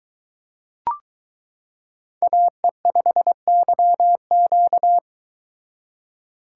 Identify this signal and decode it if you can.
Morse code, heard as a keyed tone, about 22 wpm, 700 Hz. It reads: AE5YQ